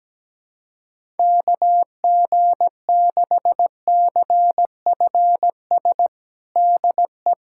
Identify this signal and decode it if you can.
Morse code, heard as a keyed tone, about 17 wpm, 700 Hz. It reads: KG6CFS DE
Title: USA Callsign DE Callsign - 6 Characters - 17wpm